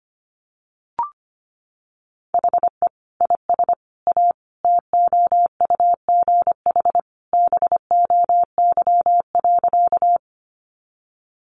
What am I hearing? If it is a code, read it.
Morse code, heard as a keyed tone, about 25 wpm, 700 Hz. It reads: HE IS A TOUGH BOY.
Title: Sentences from Top 1200 Words 25wpm